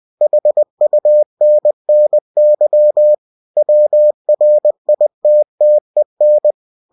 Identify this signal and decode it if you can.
Morse code, heard as a keyed tone, about 20 wpm, 600 Hz. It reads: HUNNY WRITTEN
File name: cw-books / winnie-the-pooh / winnie-20wpm-page-0037